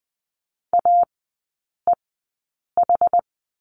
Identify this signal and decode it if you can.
Morse code, heard as a keyed tone, about 20 wpm, spaced as for 8 wpm, 700 Hz. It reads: AEH
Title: 191 Lesson 31e - Letter v - Three Characters 20wpm